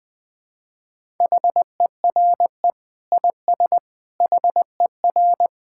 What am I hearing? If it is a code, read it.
Morse code, heard as a keyed tone, about 20 wpm, 700 Hz. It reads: HERE IS HER
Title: Sentences from Top 1600 Words 20wpm